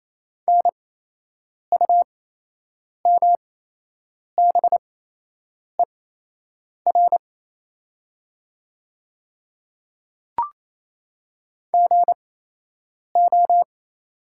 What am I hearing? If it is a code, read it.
Morse code, heard as a keyed tone, about 28 wpm, spaced as for 8 wpm, 700 Hz. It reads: NUMBER GO
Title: 8x Character Spacing - Top 100 Words - 28wpm